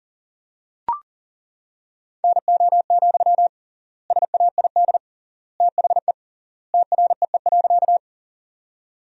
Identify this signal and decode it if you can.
Morse code, heard as a keyed tone, about 40 wpm, 700 Hz. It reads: NO, SAID THE TREE.
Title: 3x Word-Spacing - Sentences from Top 300 Words - 40wpm